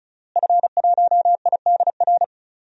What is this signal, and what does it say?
Morse code, heard as a keyed tone, about 34 wpm, 700 Hz. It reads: F1IDR
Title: DXCC-100 Call Signs - 5 Characters - Repeated 3x - 35wpm